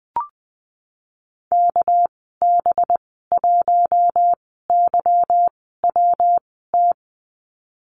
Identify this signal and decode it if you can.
Morse code, heard as a keyed tone, about 20 wpm, spaced as for 15 wpm, 700 Hz. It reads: KB1YWT
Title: 219 Lesson 35g - Number 0 - Call Signs 20wpm